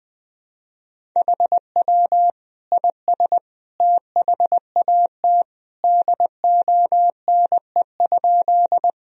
Morse code, heard as a keyed tone, about 21 wpm, 700 Hz. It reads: HW IS THAT DONE?
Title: Sentences from Top 500 Words and Abbreviations 20wpm